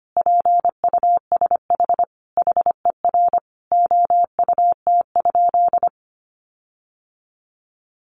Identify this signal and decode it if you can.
Morse code, heard as a keyed tone, about 25 wpm, 700 Hz. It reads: PUSH HER OUT?
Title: Sentences from Top 1300 Words 25wpm